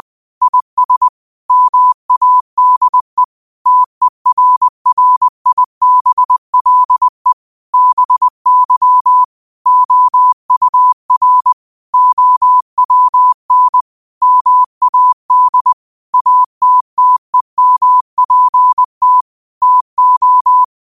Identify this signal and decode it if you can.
Morse code, heard as a keyed tone, about 20 wpm, 1000 Hz. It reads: IS MADE TERRIBLE BY OUR OWN MAD ATTEMPT TO
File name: qotdfiles / 20wpm